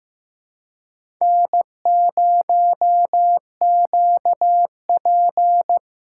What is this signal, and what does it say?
Morse code, heard as a keyed tone, about 15 wpm, 700 Hz. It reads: N0QP
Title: USA Call Signs - 4 Characters - 2x Character Spacing - 15wpm